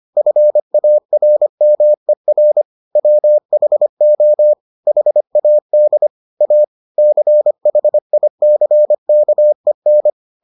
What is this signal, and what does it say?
Morse code, heard as a keyed tone, about 25 wpm, 600 Hz. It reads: FARMER WHO HAD A CHICKEN